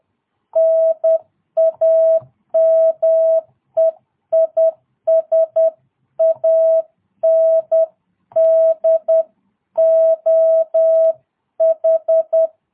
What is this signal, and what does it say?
Morse code, heard as a keyed tone, about 10 wpm, 660 Hz. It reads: NAME IS ANDOH
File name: jh8cbh / 2022cw